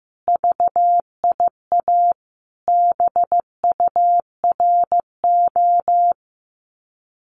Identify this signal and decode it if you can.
Morse code, heard as a keyed tone, about 14 wpm, 700 Hz. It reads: VIA BURO
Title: Common QSO Phrases Encore 15wpm